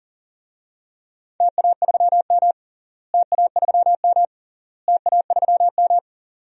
Morse code, heard as a keyed tone, about 40 wpm, 700 Hz. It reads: TA3M TA3M TA3M
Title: DXCC-100 Call Signs - 4 Characters - Repeated 3x - 40wpm